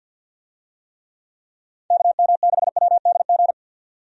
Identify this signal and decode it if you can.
Morse code, heard as a keyed tone, about 48 wpm, 700 Hz. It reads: KM6WDG
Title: USA Call Signs - 6 Characters - 50wpm